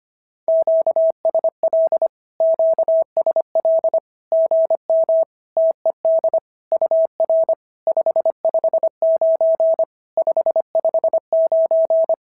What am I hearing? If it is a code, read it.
Morse code, heard as a keyed tone, about 23 wpm, 650 Hz. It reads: QSL QSL GM TED UR 559 559
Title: SOTA QSO Practice 25wpm